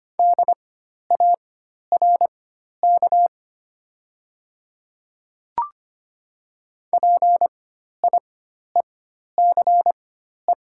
Morse code, heard as a keyed tone, about 25 wpm, spaced as for 12 wpm, 700 Hz. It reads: DARK PIECE
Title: 4x Character Spacing - Top 500 Words - 25wpm